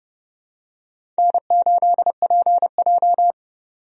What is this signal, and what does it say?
Morse code, heard as a keyed tone, about 31 wpm, 700 Hz. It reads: N8PJ
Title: USA Call Signs - 4 Characters - 30wpm